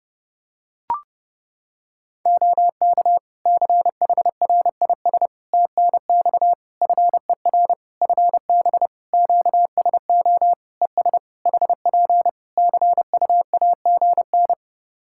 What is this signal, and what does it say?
Morse code, heard as a keyed tone, about 30 wpm, 700 Hz. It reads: OK CHRIS TNX FER FB QSO ES HP CUAGN
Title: Practice QSO - Exchange 4 - Ending - 30wpm